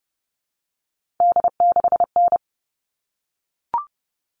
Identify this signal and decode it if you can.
Morse code, heard as a keyed tone, about 30 wpm, 700 Hz. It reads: D6N
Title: All Cut Numbers - 3-Digit 30wpm